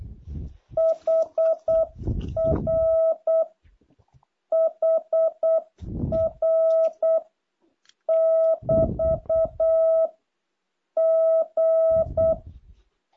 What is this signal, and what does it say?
Morse code, heard as a keyed tone, about 8 wpm, 660 Hz. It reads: HR HR <BT> G
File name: jh8cbh / 2023cw